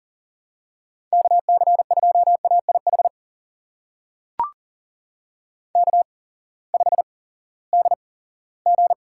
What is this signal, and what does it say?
Morse code, heard as a keyed tone, about 39 wpm, spaced as for 20 wpm, 700 Hz. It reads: KC2AIH K5DG
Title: USA Call Signs - 8x Character Spacing - 40wpm